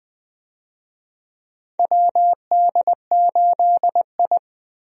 Morse code, heard as a keyed tone, about 20 wpm, 700 Hz. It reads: WD8I
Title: USA Call Signs - 4 Characters - Repeated 2x - 20wpm